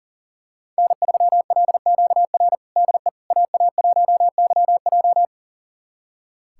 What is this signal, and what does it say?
Morse code, heard as a keyed tone, about 37 wpm, 700 Hz. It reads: N3LQR DE AA1YJ
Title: USA Callsign DE Callsign - 5 Characters - 40wpm